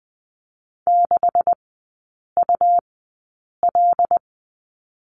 Morse code, heard as a keyed tone, about 20 wpm, spaced as for 8 wpm, 700 Hz. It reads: 6UL
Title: 133 Lesson 22e - Question Mark - Three Characters 20wpm